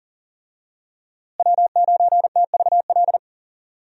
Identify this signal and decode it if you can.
Morse code, heard as a keyed tone, about 40 wpm, 700 Hz. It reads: W9TVL
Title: USA Call Signs - 5 Characters - Repeated 2x - 40wpm